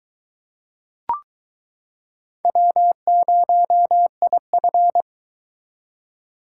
Morse code, heard as a keyed tone, about 23 wpm, 700 Hz. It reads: W0IF